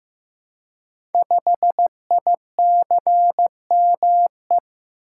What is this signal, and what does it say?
Morse code, heard as a keyed tone, about 15 wpm, 700 Hz. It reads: 5ICME
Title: Code Groups - 5 Letter:Numbers - 15wpm